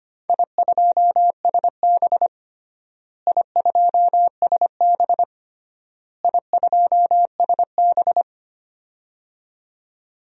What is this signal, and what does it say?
Morse code, heard as a keyed tone, about 25 wpm, 700 Hz. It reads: I2SB I2SB I2SB
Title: DXCC-100 Call Signs - 4 Characters - Repeated 3x - 25wpm